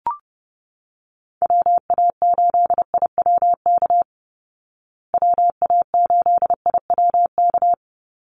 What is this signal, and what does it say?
Morse code, heard as a keyed tone, about 30 wpm, 700 Hz. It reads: WA8IWK WA8IWK
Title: Call Signs Repeated 2x - 30wpm